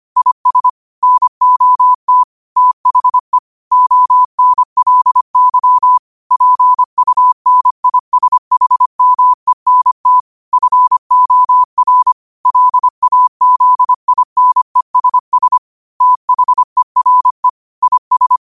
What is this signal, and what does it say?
Morse code, heard as a keyed tone, about 25 wpm, 1000 Hz. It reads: IS NOT THE ONLY PUNISHMENT FOR LAZINESS THERE IS
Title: 25 WPM morse code quotes for Thu, 29 Feb 2024